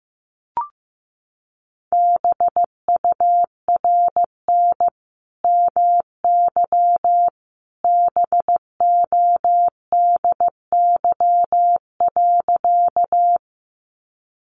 Morse code, heard as a keyed tone, about 15 wpm, 700 Hz. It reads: BURN MY BODY.
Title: Sentences from Top 1300 Words 15wpm